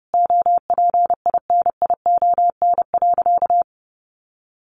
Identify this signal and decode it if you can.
Morse code, heard as a keyed tone, about 28 wpm, 700 Hz. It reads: OPINION.
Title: A Sound Of Thunder 30wpm